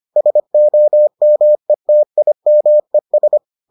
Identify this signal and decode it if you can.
Morse code, heard as a keyed tone, about 25 wpm, 600 Hz. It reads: SOMETIMES